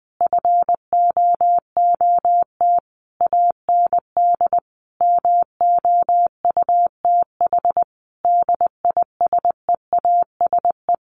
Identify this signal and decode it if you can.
Morse code, heard as a keyed tone, about 20 wpm, 700 Hz. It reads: FOOT AND MOUTH DISEASE
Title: Sentences from Top 1500 Words 20wpm